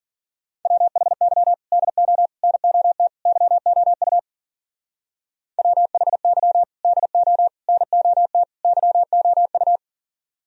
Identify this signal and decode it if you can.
Morse code, heard as a keyed tone, about 47 wpm, 700 Hz. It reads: WHY DO NOT YOU WHY DO NOT YOU
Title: Speed-Racing - Sets of 4 Words 40wpm